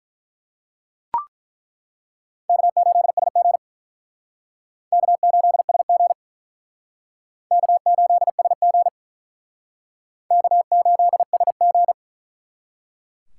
Morse code, heard as a keyed tone, about 48 wpm, 700 Hz. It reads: K8SG K8SG K8SG K8SG